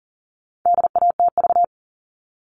Morse code, heard as a keyed tone, about 40 wpm, 700 Hz. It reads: DATV
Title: All Cut Numbers - 4-Digit 40wpm